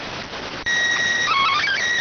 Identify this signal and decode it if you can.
SSTV (slow-scan television), PD 290.